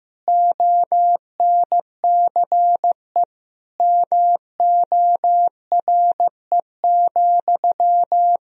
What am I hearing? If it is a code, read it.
Morse code, heard as a keyed tone, about 15 wpm, 700 Hz. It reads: ONCE MORE,
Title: Sentences from Top 300 Words and Abbreviations 15wpm